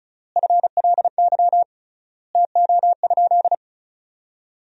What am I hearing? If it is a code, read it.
Morse code, heard as a keyed tone, about 36 wpm, 700 Hz. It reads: FLY TO?
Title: 3x Word-Spacing - Sentences from Top 500 Words - 35wpm